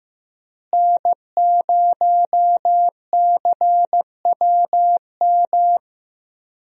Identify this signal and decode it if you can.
Morse code, heard as a keyed tone, about 15 wpm, 700 Hz. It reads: N0CWM
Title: USA Call Signs - 5 Characters - Repeated 3x - 15wpm